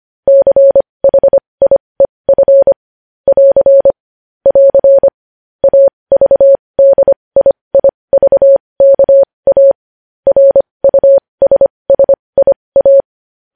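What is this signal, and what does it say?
Morse code, heard as a keyed tone, about 25 wpm, spaced as for 20 wpm, 550 Hz. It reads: CHIEF <AR> <AR> AVDIIVKA RUSSIA
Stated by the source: Morse code news headlines sent at 20 wpm.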